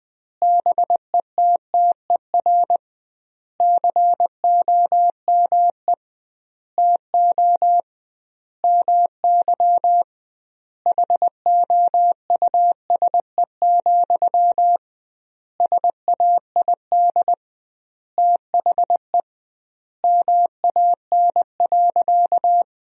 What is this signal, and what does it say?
Morse code, heard as a keyed tone, about 20 wpm, 700 Hz. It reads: BETTER COME TO MY HOUSE, SAID THE MAN.
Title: 2x Word-Spacing - Sentences from Top 400 Words - 20wpm